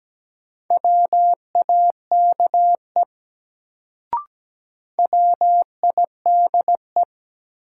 Morse code, heard as a keyed tone, about 17 wpm, 700 Hz. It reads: WAKE WIDE
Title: Sets of 4 Words - New Word Drill - 17wpm